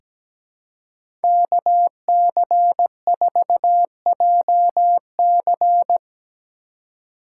Morse code, heard as a keyed tone, about 17 wpm, 700 Hz. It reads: KC4JC
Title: USA Call Signs - 3x Character Spacing - 17wpm